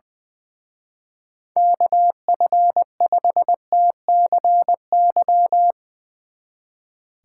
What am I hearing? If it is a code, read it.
Morse code, heard as a keyed tone, about 22 wpm, 700 Hz. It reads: KF5TCY
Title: Speed Racing - USA 2x3 Call Signs 20wpm